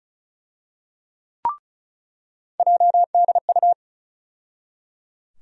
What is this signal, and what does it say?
Morse code, heard as a keyed tone, about 36 wpm, 700 Hz. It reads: JDU
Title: USA Call Sign Postfixes - 3 Letters 35wpm